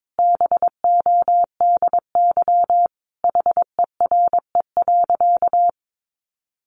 Morse code, heard as a keyed tone, about 22 wpm, 700 Hz. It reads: BODY HERE.
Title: Sentences from Top 1100 Words 22wpm